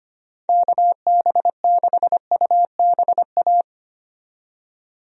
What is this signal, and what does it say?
Morse code, heard as a keyed tone, about 25 wpm, 700 Hz. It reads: KB6UBA
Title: USA Call Signs - 6 Characters - 25wpm